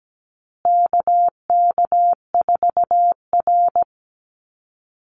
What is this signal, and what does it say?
Morse code, heard as a keyed tone, about 17 wpm, 700 Hz. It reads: KK4R
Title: Call Signs Repeated 2x - 17wpm